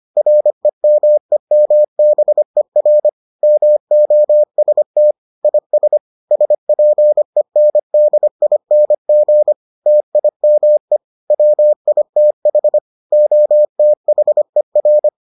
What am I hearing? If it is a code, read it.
Morse code, heard as a keyed tone, about 25 wpm, 600 Hz. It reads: REMEMBER MOST IS SPENDING TIME WITH OTHER